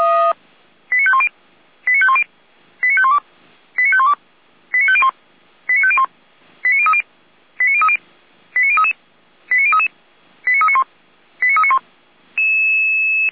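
Five-tone selective call (ZVEI), heard as quick runs of five tones, tones of about 70 ms. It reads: 89510 89510 89512 89512 89501 89501 89030 89030 89033 89033 89391 89391
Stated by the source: ZVEI five-tone selective call received by SDR.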